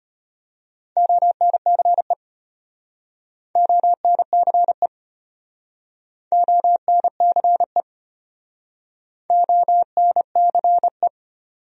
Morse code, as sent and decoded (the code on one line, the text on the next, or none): --- -. -.-. / --- -. -.-. / --- -. -.-. / --- -. -.-. .
ONCE ONCE ONCE ONCE